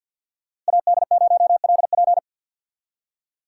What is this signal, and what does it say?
Morse code, heard as a keyed tone, about 63 wpm, 700 Hz. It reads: AD0LP